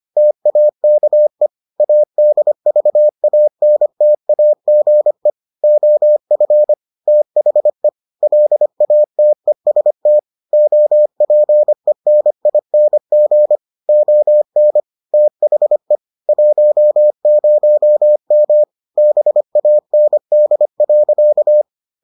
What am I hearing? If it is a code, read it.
Morse code, heard as a keyed tone, about 25 wpm, 600 Hz. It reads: TAKE ADVANTAGE OF THE LATEST OPENING ON THE 10M BAND.